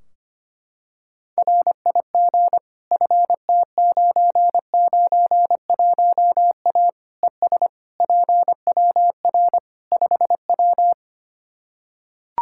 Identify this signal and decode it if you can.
Morse code, heard as a keyed tone, about 25 wpm, 700 Hz. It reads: RIG FT991A ES PWR 5W
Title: Practice QSO - Exchange 2 - Info - 25wpm